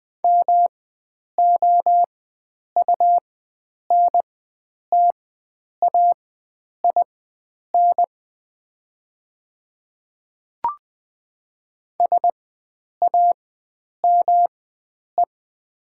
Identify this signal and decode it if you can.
Morse code, heard as a keyed tone, about 20 wpm, spaced as for 9 wpm, 700 Hz. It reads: MOUNTAIN SAME
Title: 4x Character Spacing - Top 500 Words - 20wpm